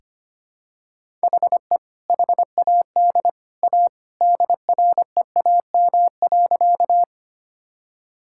Morse code, heard as a keyed tone, about 25 wpm, 700 Hz. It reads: HE HAD A DREAM.
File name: practice / 25wpm / Sentences with 2-Word Phrases - 25wpm